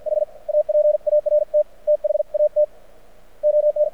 Morse code, heard as a keyed tone, about 55 wpm, spaced as for 47 wpm, 600 Hz. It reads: H APART THAT ON
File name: archived / CRSnet